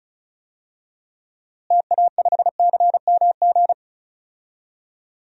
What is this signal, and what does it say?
Morse code, heard as a keyed tone, about 36 wpm, 700 Hz. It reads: TA5CMG
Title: DXCC-100 Call Signs - 6 Characters - Repeated 2x - 35wpm